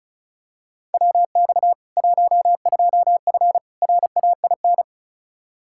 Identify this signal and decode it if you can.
Morse code, heard as a keyed tone, about 36 wpm, 700 Hz. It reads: WX 12F RAIN
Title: K1USN SST QSO Practice 35wpm